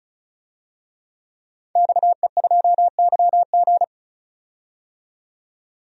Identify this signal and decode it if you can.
Morse code, heard as a keyed tone, about 34 wpm, 700 Hz. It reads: XE2YG
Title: DXCC-100 Call Signs - 5 Characters - 35wpm